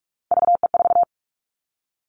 Morse code, heard as a keyed tone, about 46 wpm, 700 Hz. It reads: VE4
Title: Callsign Prefixes for Top 50 DXCC - 45wpm